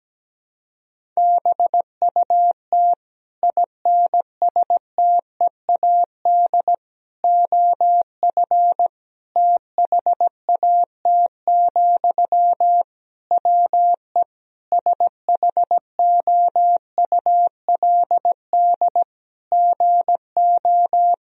Morse code, heard as a keyed tone, about 17 wpm, 700 Hz. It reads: BUT INSTEAD OF THAT, WE SHOULD GO
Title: Sentences from Top 1000 Words - Review Part 4 - 17wpm